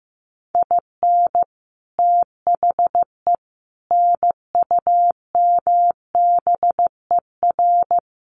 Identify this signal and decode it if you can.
Morse code, heard as a keyed tone, about 15 wpm, 700 Hz. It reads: IN THE NUMBER
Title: Sets of 4 Words 15wpm